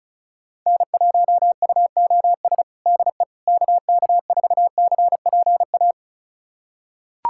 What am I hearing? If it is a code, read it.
Morse code, heard as a keyed tone, about 36 wpm, 700 Hz. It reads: N1UOS DE KK4CPA
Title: USA Callsign DE Callsign 35wpm